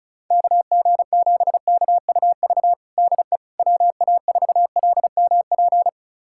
Morse code, heard as a keyed tone, about 34 wpm, 700 Hz. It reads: KG7KUV DE WA4LMP
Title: USA Callsign DE Callsign - 6 Characters - 35wpm